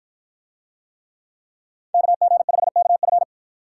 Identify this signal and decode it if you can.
Morse code, heard as a keyed tone, about 52 wpm, 700 Hz. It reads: KG5KF